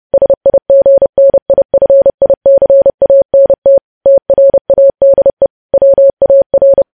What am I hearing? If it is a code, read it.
Morse code, heard as a keyed tone, about 30 wpm, 550 Hz. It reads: SIGNIFICANT TRADE WAR
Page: news headlines in morse code at 30 wpm